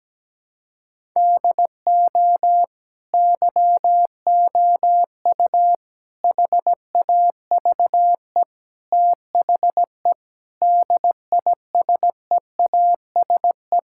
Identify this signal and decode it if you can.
Morse code, heard as a keyed tone, about 17 wpm, 700 Hz. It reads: DO YOU HAVE THE DISEASE
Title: Sentences from Top 1000 Words - Review Part 6 - 17wpm